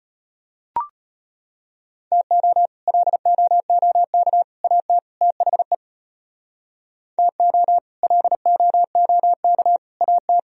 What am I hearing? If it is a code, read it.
Morse code, heard as a keyed tone, about 31 wpm, 700 Hz. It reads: TO LOOK AT THE TO LOOK AT
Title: Speed-Racing - Sets of 4 Words 25wpm